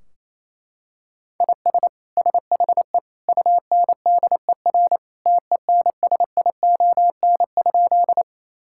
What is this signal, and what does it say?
Morse code, heard as a keyed tone, about 28 wpm, 700 Hz. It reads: IS SHE UNDER TENSION?
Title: Sentences from Top 1600 Words 28wpm